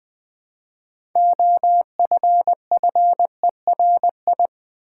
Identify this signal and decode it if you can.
Morse code, heard as a keyed tone, about 20 wpm, 700 Hz. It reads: OFFERI
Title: Build A Word - 2 Letters At A Time - 8-Letter Words - 20wpm